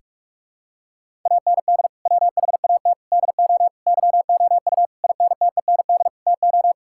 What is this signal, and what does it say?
Morse code, heard as a keyed tone, about 43 wpm, 700 Hz. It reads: AND WHAT DO YOU INTEND TO
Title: Sentences from Top 1600 Words 45wpm